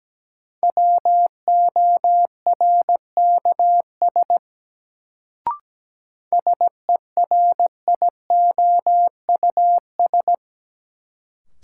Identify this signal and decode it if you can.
Morse code, heard as a keyed tone, about 17 wpm, 700 Hz. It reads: WORKS SERIOUS
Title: Sentences from Top 1100 Words - New Word Drill Alt - 17wpm